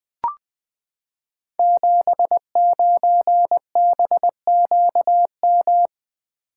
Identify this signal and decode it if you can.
Morse code, heard as a keyed tone, about 20 wpm, 700 Hz. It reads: 79BQM
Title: Code Groups - 5 Letter:Numbers - 20wpm